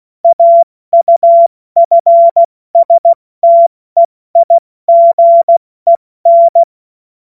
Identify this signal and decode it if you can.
Morse code, heard as a keyed tone, about 16 wpm, 680 Hz. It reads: AUFSTEIGEN
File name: Morsekurs / Witze / Witz 09. Dezember 2025 70 F80